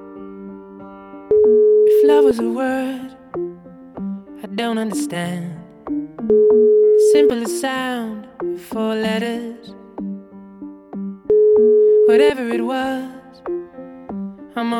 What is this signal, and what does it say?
Ringback tone (Europe, 425 Hz), with speech.